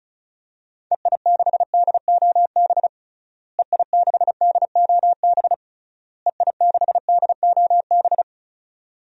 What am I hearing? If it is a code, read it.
Morse code, heard as a keyed tone, about 36 wpm, 700 Hz. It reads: EI6DOB EI6DOB EI6DOB
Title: DXCC-100 Call Signs - 6 Characters - Repeated 3x - 35wpm